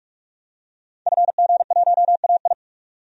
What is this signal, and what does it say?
Morse code, heard as a keyed tone, about 46 wpm, 700 Hz. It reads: FG1AI
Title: DXCC-100 Call Signs - 5 Characters - 45wpm